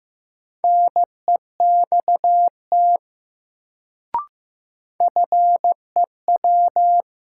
Morse code, heard as a keyed tone, about 15 wpm, 700 Hz. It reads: NEXT FEW
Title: Sets of 2 Words - New Word Drill - 15wpm